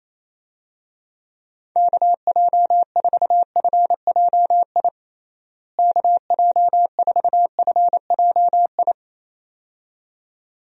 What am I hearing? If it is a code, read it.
Morse code, heard as a keyed tone, about 27 wpm, 700 Hz. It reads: KJ4FJI KJ4FJI
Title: USA Call Signs - 6 Characters - Repeated 2x - 28wpm